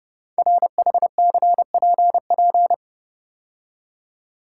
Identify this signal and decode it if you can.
Morse code, heard as a keyed tone, about 30 wpm, 700 Hz. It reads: RHCPP
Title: Code Groups - 5 Letters - 30wpm